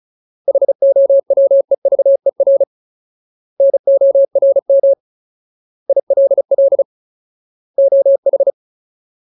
Morse code, heard as a keyed tone, about 35 wpm, 550 Hz. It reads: HOWEVER NORM ILL OH
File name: Ultimate Word List / 35WPM Ultimate